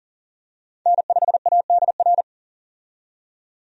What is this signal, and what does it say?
Morse code, heard as a keyed tone, about 40 wpm, 700 Hz. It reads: N5ADR
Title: USA Call Signs - 5 Characters - 40wpm